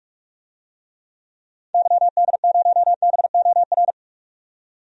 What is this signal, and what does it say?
Morse code, heard as a keyed tone, about 44 wpm, 700 Hz. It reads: YD0BOR
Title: DXCC-100 Call Signs - 6 Characters - 45wpm